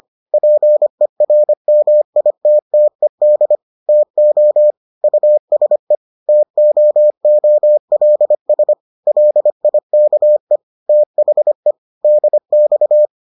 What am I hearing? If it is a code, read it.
Morse code, heard as a keyed tone, about 25 wpm, 600 Hz. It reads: PERMITTED TO USE TOOLS LIKE THE DX